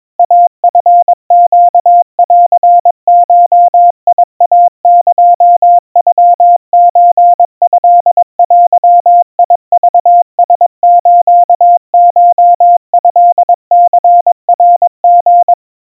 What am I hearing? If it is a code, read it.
Morse code (Japanese Wabun code), heard as a keyed tone, about 22 wpm, 700 Hz. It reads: イチネンゴイエノソトデクラスコトニナリ
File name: 80 cpm 700 Hz Crispy